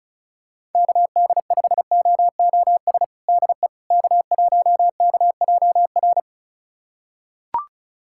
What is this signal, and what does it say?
Morse code, heard as a keyed tone, about 36 wpm, 700 Hz. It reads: KD5OOS DE K1KJR
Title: USA Callsign DE Callsign 35wpm